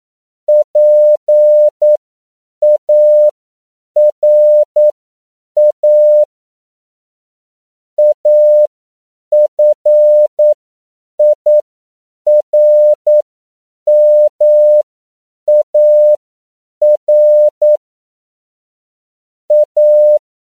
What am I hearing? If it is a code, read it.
Morse code, heard as a keyed tone, about 9 wpm, spaced as for 7 wpm, 600 Hz. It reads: PARA AFIRMAR A